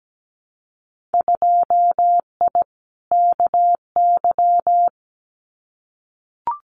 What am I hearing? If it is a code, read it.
Morse code, heard as a keyed tone, about 17 wpm, 700 Hz. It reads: 2I KY
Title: Winter Field Day Exchanges 17wpm